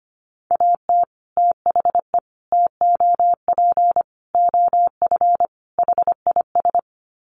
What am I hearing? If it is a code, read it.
Morse code, heard as a keyed tone, about 25 wpm, 700 Hz. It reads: AT THE TOP OF HIS
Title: Sets of 5 Words Encore 25wpm